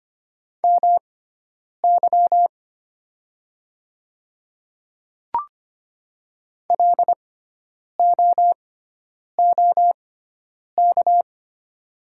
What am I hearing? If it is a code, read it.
Morse code, heard as a keyed tone, about 25 wpm, spaced as for 9 wpm, 700 Hz. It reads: MY LOOK
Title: 6x Character Spacing - Top 100 Words - 25wpm